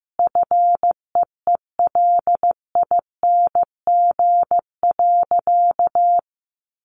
Morse code, heard as a keyed tone, about 15 wpm, 700 Hz. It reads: FEELING.
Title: Sentences from Top 1200 Words 15wpm